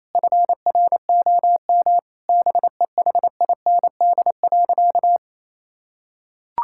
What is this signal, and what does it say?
Morse code, heard as a keyed tone, about 28 wpm, 700 Hz. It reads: FROM BEHIND.
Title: Sentences from Top 1600 Words 28wpm